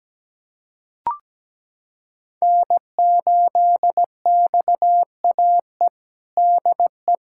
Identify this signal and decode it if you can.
Morse code, heard as a keyed tone, about 17 wpm, 700 Hz. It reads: N8XAE DE
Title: USA Callsign DE Callsign - 5 Characters - 17wpm